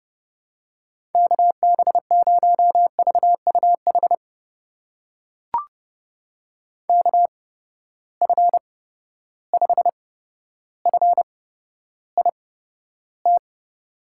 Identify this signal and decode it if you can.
Morse code, heard as a keyed tone, about 30 wpm, spaced as for 13 wpm, 700 Hz. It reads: KB0VUH KF5FIT
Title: USA Call Signs - 8x Character Spacing - 30wpm